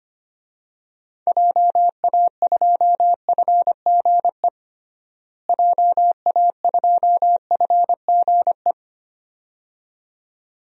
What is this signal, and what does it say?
Morse code, heard as a keyed tone, about 25 wpm, 700 Hz. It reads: JA2FGE JA2FGE
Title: DXCC-100 Call Signs - 6 Characters - Repeated 2x - 25wpm